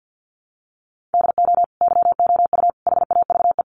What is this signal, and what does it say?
Morse code, heard as a keyed tone, about 50 wpm, 700 Hz. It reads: DO YOU HAVE
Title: Sentences from Top 1400 Words 50wpm